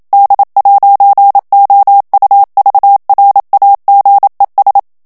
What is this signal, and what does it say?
Morse code, heard as a keyed tone, about 27 wpm, 800 Hz. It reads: D'OUVRAGES
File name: FRANCAIS TEXTE CLAIR 2 VITESSE 26 MOTS MINUTES